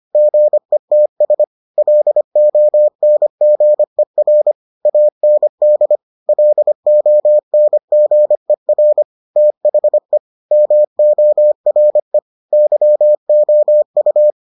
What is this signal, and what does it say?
Morse code, heard as a keyed tone, about 25 wpm, 600 Hz. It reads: GETS LONGER AND LONGER THE MORE YOU